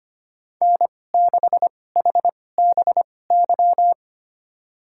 Morse code, heard as a keyed tone, about 25 wpm, spaced as for 18 wpm, 700 Hz. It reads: N6HBY
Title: USA Call Signs - 2x Character Spacing - 25wpm